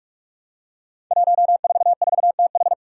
Morse code, heard as a keyed tone, about 44 wpm, 700 Hz. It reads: 144TH
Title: Five Challenging Letter-Number Combinations 45wpm